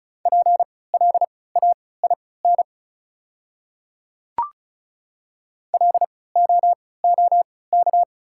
Morse code, heard as a keyed tone, about 35 wpm, spaced as for 20 wpm, 700 Hz. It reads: PLAIN LOOK
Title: 3x Character Spacing - Top 500 Words - 35wpm